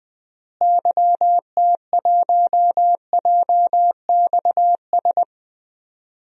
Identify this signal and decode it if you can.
Morse code, heard as a keyed tone, about 21 wpm, 700 Hz. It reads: YT1JXS
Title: DXCC-100 Call Signs - 6 Characters - Repeated 2x - 20wpm